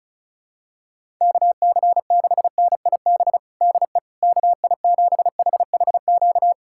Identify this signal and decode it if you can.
Morse code, heard as a keyed tone, about 36 wpm, 700 Hz. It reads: KC6NIB DE KI7HHQ
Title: USA Callsign DE Callsign - 6 Characters - 35wpm